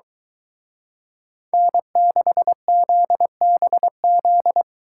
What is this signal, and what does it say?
Morse code, heard as a keyed tone, about 23 wpm, 700 Hz. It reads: N6ZBZ